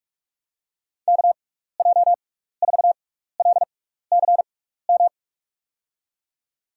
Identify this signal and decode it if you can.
Morse code, heard as a keyed tone, about 46 wpm, spaced as for 16 wpm, 700 Hz. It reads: KJ4LCM
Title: USA Call Signs - 6x Character Spacing - 45wpm